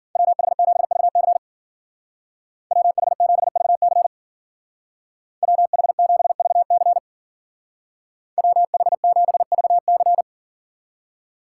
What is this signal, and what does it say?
Morse code, heard as a keyed tone, about 52 wpm, 700 Hz. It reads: WH7VC WH7VC WH7VC WH7VC